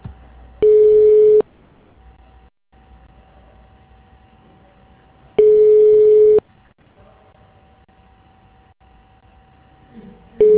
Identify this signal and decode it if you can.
Ringback tone (Europe, 425 Hz).